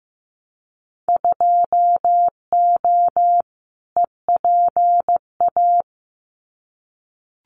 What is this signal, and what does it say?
Morse code, heard as a keyed tone, about 15 wpm, 700 Hz. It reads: 2O EPA
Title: Winter Field Day Exchanges Encore 15wpm